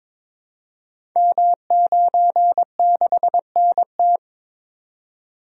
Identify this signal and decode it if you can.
Morse code, heard as a keyed tone, about 22 wpm, 700 Hz. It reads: M96NT
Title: Code Groups - 5 Letter:Numbers - 22wpm